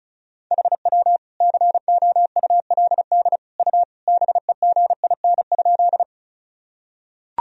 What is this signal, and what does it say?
Morse code, heard as a keyed tone, about 36 wpm, 700 Hz. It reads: HW COULD U BEGIN?
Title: Sentences from Top 300 Words and Abbreviations 35wpm